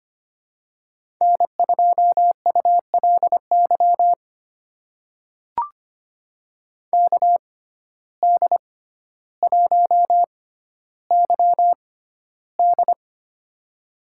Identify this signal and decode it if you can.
Morse code, heard as a keyed tone, about 25 wpm, spaced as for 13 wpm, 700 Hz. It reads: N2ULY KD1YD
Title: USA Call Signs - 5 Characters - 6x Character Spacing - 25wpm